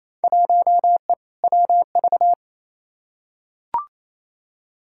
Morse code, heard as a keyed tone, about 28 wpm, 700 Hz. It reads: 1E WV
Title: Field Day Exchanges Encore - 28wpm